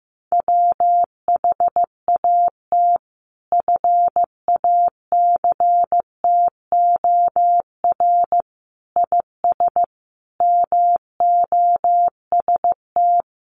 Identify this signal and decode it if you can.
Morse code, heard as a keyed tone, about 15 wpm, 700 Hz. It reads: WHAT FACTOR IS MOST
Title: Sentences from Top 1300 Words 15wpm